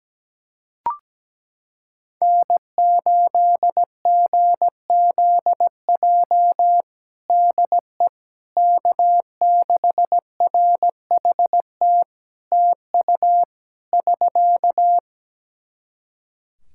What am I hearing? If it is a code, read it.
Morse code, heard as a keyed tone, about 17 wpm, 700 Hz. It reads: N8GZJ DE K6RHT TU <SK>
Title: Practice QSO - Exchange 4 - Ending - 17wpm